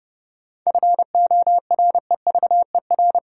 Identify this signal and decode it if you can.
Morse code, heard as a keyed tone, about 30 wpm, 700 Hz. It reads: FOREVER
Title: Sentences from Top 1600 Words - New Word Drill - 30wpm